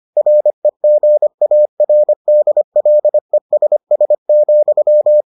Morse code, heard as a keyed tone, about 25 wpm, 600 Hz. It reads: REGARDLESS,